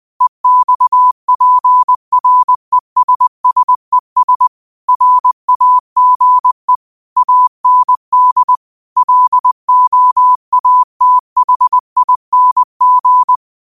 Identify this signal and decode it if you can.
Morse code, heard as a keyed tone, about 20 wpm, 1000 Hz. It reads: EXPRESSES RAGE AND LOATHING
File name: qotdfiles / 20wpm